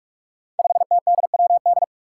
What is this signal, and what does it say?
Morse code, heard as a keyed tone, about 47 wpm, 700 Hz. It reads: 5TDWD
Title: Five Challenging Letter-Number Combinations 45wpm